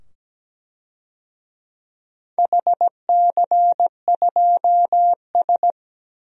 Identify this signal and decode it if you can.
Morse code, heard as a keyed tone, about 17 wpm, 700 Hz. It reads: HC2S
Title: DXCC-100 Call Signs - 4 Characters - 17wpm